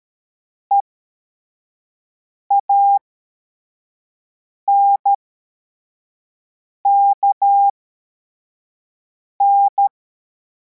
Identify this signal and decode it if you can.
Morse code, heard as a keyed tone, about 13 wpm, spaced as for 4 wpm, 800 Hz. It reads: EANKN